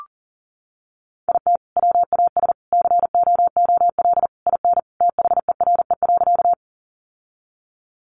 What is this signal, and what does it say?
Morse code, heard as a keyed tone, about 40 wpm, 700 Hz. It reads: IT WAS COOL IN THERE.
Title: Sentences from Top 1200 Words 40wpm